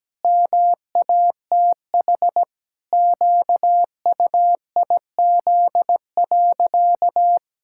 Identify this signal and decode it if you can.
Morse code, heard as a keyed tone, about 17 wpm, 700 Hz. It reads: MATH QUIZ.
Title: Pangram Challenge v2 - 17wpm